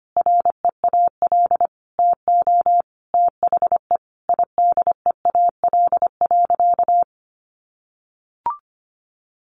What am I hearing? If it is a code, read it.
Morse code, heard as a keyed tone, about 25 wpm, 700 Hz. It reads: REAL TO THE IDEAL.
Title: Sentences from Top 1400 Words 25wpm